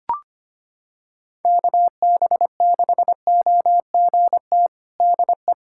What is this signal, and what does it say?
Morse code, heard as a keyed tone, about 25 wpm, 700 Hz. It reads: KB6OGT DE
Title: USA Callsign DE Callsign - 6 Characters - 25wpm